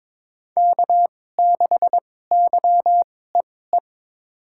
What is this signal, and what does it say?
Morse code, heard as a keyed tone, about 22 wpm, spaced as for 16 wpm, 700 Hz. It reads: K6YEE
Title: USA Call Signs - 5 Characters - 2x Character Spacing - 22wpm